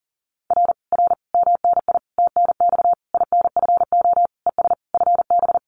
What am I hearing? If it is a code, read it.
Morse code, heard as a keyed tone, about 40 wpm, 700 Hz. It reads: R R MNI TNX INFO ES FB
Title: Common QSO Phrases Encore 40wpm